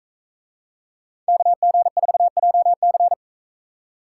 Morse code, heard as a keyed tone, about 43 wpm, 700 Hz. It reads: KG4JC